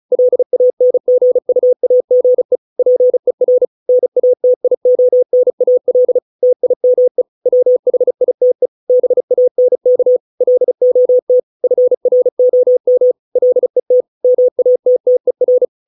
Morse code, heard as a keyed tone, about 35 wpm, 500 Hz. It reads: LANGUAGE PER NATIONAL TIME WHITE BANK LOT FROM LET MATTER